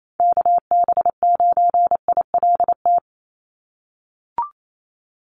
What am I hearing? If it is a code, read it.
Morse code, heard as a keyed tone, about 28 wpm, 700 Hz. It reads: KB9ILT
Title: NA SOTA Activator Callsigns 28wpm